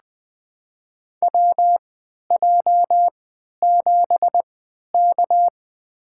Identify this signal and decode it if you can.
Morse code, heard as a keyed tone, about 20 wpm, spaced as for 11 wpm, 700 Hz. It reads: WJ7K
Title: USA Call Signs - 4 Characters - 3x Character Spacing - 20wpm